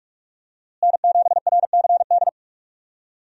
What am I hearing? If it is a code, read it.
Morse code, heard as a keyed tone, about 46 wpm, 700 Hz. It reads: N7RCD
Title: USA Call Signs - 5 Characters - Repeated 3x - 45wpm